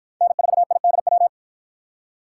Morse code, heard as a keyed tone, about 53 wpm, 700 Hz. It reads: N4IDW